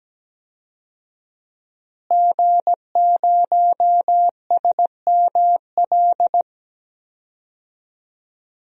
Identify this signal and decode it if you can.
Morse code, heard as a keyed tone, about 17 wpm, 700 Hz. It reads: G0SML
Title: DXCC-100 Call Signs - 5 Characters - Repeated 2x - 17wpm